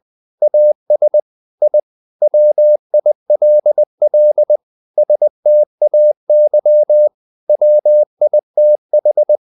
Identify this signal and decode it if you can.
Morse code, heard as a keyed tone, about 20 wpm, 600 Hz. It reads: AS I WILL STAY WITH
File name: the old man and the sea / the old man and the sea part 1 20wpm